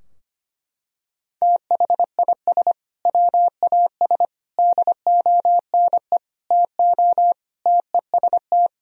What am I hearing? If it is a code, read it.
Morse code, heard as a keyed tone, about 25 wpm, 700 Hz. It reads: THIS WAS DONE TO TEST
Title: Sentences from Top 1000 Words - Review Part 3 - 25wpm